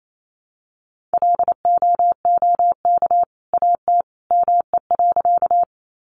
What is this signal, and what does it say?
Morse code, heard as a keyed tone, about 28 wpm, 700 Hz. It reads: LOOK AT ME.
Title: Sentences from Top 200 Words 28wpm